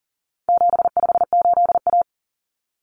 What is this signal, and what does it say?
Morse code, heard as a keyed tone, about 40 wpm, 700 Hz. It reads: 758A
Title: Common Cut Numbers - 4-Digit 40wpm